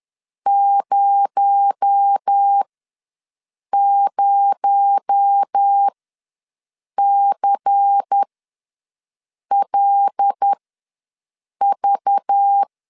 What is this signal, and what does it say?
Morse code, heard as a keyed tone, about 10 wpm, spaced as for 6 wpm, 800 Hz. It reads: 00CLV
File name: LEZIONI / 6 PKX0 / 6 PKX0 ws 10wpm 4220